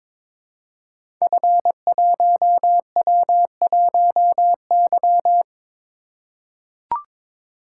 Morse code, heard as a keyed tone, about 22 wpm, 700 Hz. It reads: F1W1Y
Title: Code Groups - 5 Letter:Numbers - 22wpm